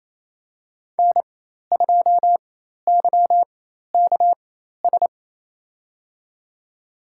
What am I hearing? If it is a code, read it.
Morse code, heard as a keyed tone, about 28 wpm, spaced as for 13 wpm, 700 Hz. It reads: N2YKS
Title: USA Call Signs - 4x Character Spacing - 28wpm